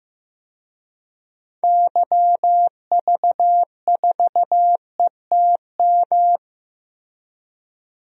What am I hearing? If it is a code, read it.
Morse code, heard as a keyed tone, about 15 wpm, 700 Hz. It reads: YV4ETM
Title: DXCC-100 Call Signs - 6 Characters - Repeated 3x - 15wpm